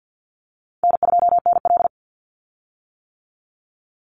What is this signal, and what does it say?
Morse code, heard as a keyed tone, about 50 wpm, 700 Hz. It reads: N2NL
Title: Easy Call Signs - 50wpm